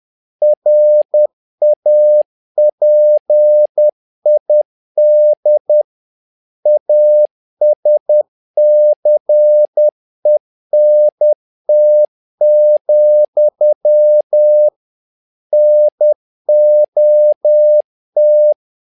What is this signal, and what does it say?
Morse code, heard as a keyed tone, about 10 wpm, 600 Hz. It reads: RAPID ASCENT, NOT
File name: learning center / tower trouble / tower trouble 10wpm